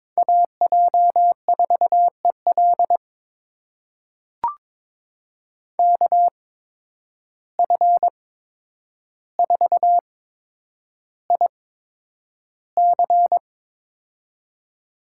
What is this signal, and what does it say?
Morse code, heard as a keyed tone, about 22 wpm, spaced as for 9 wpm, 700 Hz. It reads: AJ4EL KF4IC
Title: USA Call Signs - 5 Characters - 8x Character Spacing - 22wpm